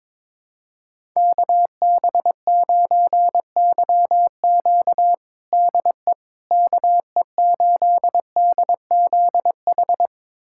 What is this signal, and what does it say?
Morse code, heard as a keyed tone, about 22 wpm, 700 Hz. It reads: KB9YQ DE KE8DZH
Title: USA Callsign DE Callsign 22wpm